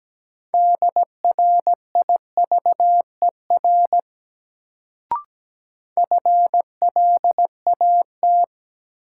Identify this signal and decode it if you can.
Morse code, heard as a keyed tone, about 17 wpm, 700 Hz. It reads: DRIVER FLAT
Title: Sentences from Top 1000 Words - New Word Drill - 17wpm